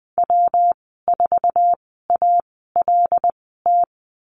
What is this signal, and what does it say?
Morse code, heard as a keyed tone, about 20 wpm, spaced as for 15 wpm, 700 Hz. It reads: W4ALT
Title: 186 Lesson 30g - Letter b - Call Signs 20wpm